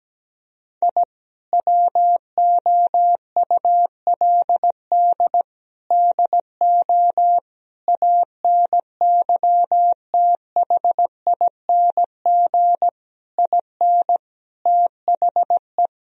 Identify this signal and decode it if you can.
Morse code, heard as a keyed tone, about 17 wpm, 700 Hz. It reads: I WOULD DO ANYTHING IN THE
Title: Sentences from Top 1000 Words - Review Part 2 - 17wpm